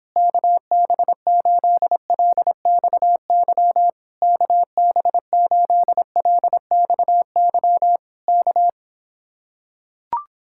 Morse code, heard as a keyed tone, about 26 wpm, 700 Hz. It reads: KB8LXY KB8LXY K